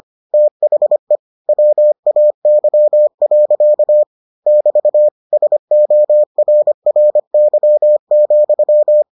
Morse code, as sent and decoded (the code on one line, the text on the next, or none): - .... / .-- .- -.-- .-.-.- / -...- / ... --- .-. .-. -.-- --..--
THE WAY. <BT> SORRY,